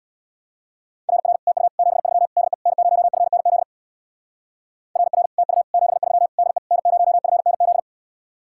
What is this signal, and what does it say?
Morse code, heard as a keyed tone, about 67 wpm, 700 Hz. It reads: RR TU 73 DE N0FNZ RR TU 73 DE N0FNZ